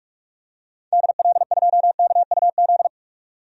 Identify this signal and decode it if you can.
Morse code, heard as a keyed tone, about 44 wpm, 700 Hz. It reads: DL2KUZ
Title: DXCC-100 Call Signs - 6 Characters - 45wpm